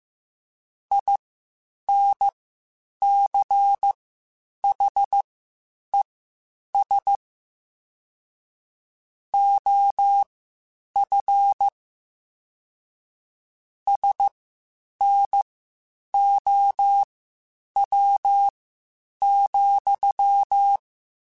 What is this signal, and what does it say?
Morse code, heard as a keyed tone, about 15 wpm, spaced as for 8 wpm, 800 Hz. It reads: INCHES OF SNOW,
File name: (spaced) / QSOs Spaced-mp3s / WA7QOX 1 @13 sp